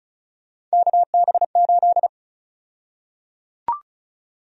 Morse code, heard as a keyed tone, about 34 wpm, 700 Hz. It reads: KB8
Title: USA Call Sign Prefixes - 35wpm